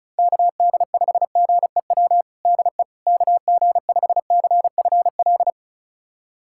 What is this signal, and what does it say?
Morse code, heard as a keyed tone, about 36 wpm, 700 Hz. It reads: KD5GEW DE KG5CFL
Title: USA Callsign DE Callsign - 6 Characters - 35wpm